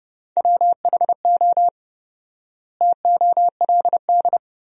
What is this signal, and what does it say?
Morse code, heard as a keyed tone, about 30 wpm, 700 Hz. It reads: WHO TOLD
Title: 4x Word-Spacing - Sentences from Top 400 Words - 30wpm